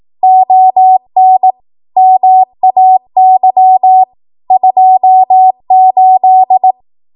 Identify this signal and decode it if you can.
Morse code, heard as a keyed tone, about 18 wpm, 750 Hz. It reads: ON MAY 28
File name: Morse / 18WPM